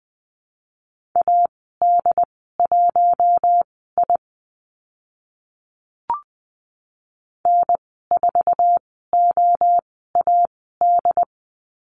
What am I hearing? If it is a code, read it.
Morse code, heard as a keyed tone, about 20 wpm, spaced as for 14 wpm, 700 Hz. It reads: AD1I N4OAD
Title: 069 Lesson 12g - Letter d - Call Signs 20wpm